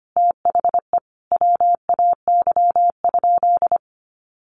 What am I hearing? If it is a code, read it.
Morse code, heard as a keyed tone, about 25 wpm, 700 Hz. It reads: THE WAY?
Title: Sentences from Top 1100 Words 25wpm